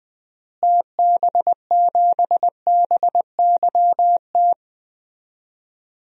Morse code, heard as a keyed tone, about 20 wpm, 700 Hz. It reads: TB7BYT
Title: DXCC-100 Call Signs - 6 Characters - Repeated 3x - 20wpm